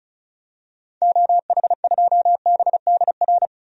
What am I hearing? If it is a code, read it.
Morse code, heard as a keyed tone, about 34 wpm, 700 Hz. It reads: OH2BDR
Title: DXCC-100 Call Signs - 6 Characters - Repeated 2x - 35wpm